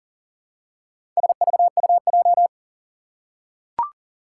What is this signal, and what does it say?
Morse code, heard as a keyed tone, about 39 wpm, 700 Hz. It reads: SVUJ
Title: Four Challenging Letter-Number Combinations 40wpm